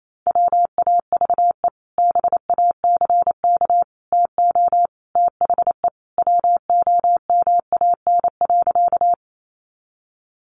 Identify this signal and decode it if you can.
Morse code, heard as a keyed tone, about 28 wpm, 700 Hz. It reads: WAVE BACK TO THE WOMAN.
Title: Sentences from Top 1400 Words 28wpm